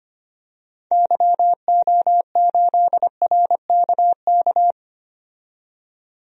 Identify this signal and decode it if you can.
Morse code, heard as a keyed tone, about 25 wpm, 700 Hz. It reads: YO8RKK
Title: DXCC-100 Call Signs - 6 Characters - 25wpm